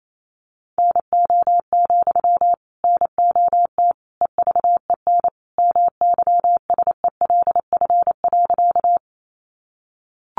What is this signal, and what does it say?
Morse code, heard as a keyed tone, about 28 wpm, 700 Hz. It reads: NO, NOT EVEN MYSELF.
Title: Sentences from Top 1300 Words 28wpm